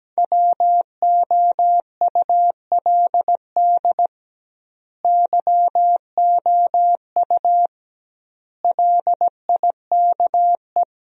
Morse code, heard as a keyed tone, about 17 wpm, 700 Hz. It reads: WOULD YOU LIKE
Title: 2x Word-Spacing - Sentences from Top 100 Words - 17wpm